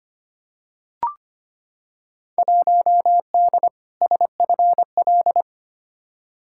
Morse code, heard as a keyed tone, about 25 wpm, 700 Hz. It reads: 1D SFL
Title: Field Day Exchanges - 25wpm